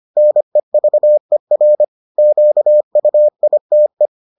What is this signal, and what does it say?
Morse code, heard as a keyed tone, about 25 wpm, 600 Hz. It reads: NEVER QUITE